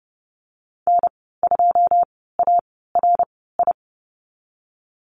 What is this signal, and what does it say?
Morse code, heard as a keyed tone, about 30 wpm, spaced as for 17 wpm, 700 Hz. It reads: N2ARI
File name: Lesson 14g - Number 2 - Call Signs 30wpm